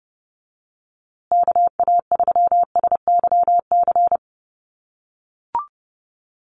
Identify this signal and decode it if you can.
Morse code, heard as a keyed tone, about 30 wpm, 700 Hz. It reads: KA3SYC
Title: USA Call Signs Repeated 2x 30wpm